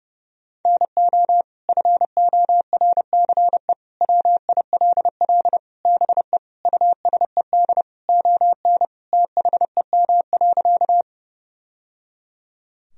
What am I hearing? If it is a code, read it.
Morse code, heard as a keyed tone, about 30 wpm, 700 Hz. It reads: NO FORCE WILL BE USED ON THEM.
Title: Sentences from Top 1000 Words - Review Part 3 - 30wpm